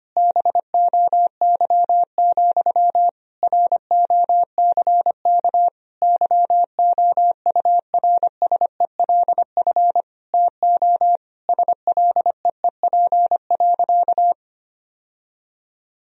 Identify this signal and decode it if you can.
Morse code, heard as a keyed tone, about 25 wpm, 700 Hz. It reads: BOY, ROCK YOURSELF TO SLEEP.
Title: Sentences from Top 1000 Words - Review Part 2 - 25wpm